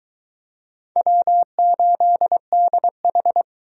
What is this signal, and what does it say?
Morse code, heard as a keyed tone, about 20 wpm, 700 Hz. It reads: W8DH